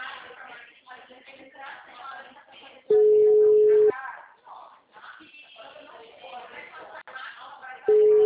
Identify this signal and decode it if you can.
Ringback tone (Europe, 425 Hz), with speech.